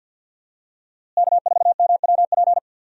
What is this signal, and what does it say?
Morse code, heard as a keyed tone, about 46 wpm, 700 Hz. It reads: K4MWP
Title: USA Call Signs - 5 Characters - Repeated 3x - 50wpm